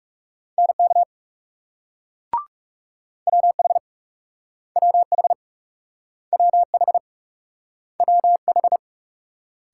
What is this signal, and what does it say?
Morse code, heard as a keyed tone, about 38 wpm, 700 Hz. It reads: NK WH WH WH WH